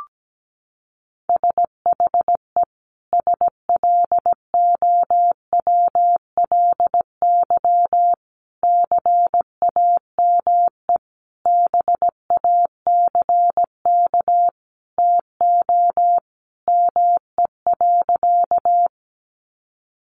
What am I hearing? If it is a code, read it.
Morse code, heard as a keyed tone, about 17 wpm, 700 Hz. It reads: SHE SLOWLY CAME BACK TO ME.
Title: Sentences from Top 1100 Words 17wpm